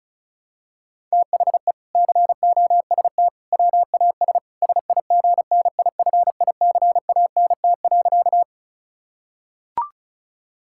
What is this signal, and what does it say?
Morse code, heard as a keyed tone, about 34 wpm, 700 Hz. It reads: THE COST WAS SIGNIFICANT.
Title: Sentences from Top 1000 Words - Review Part 6 - 35wpm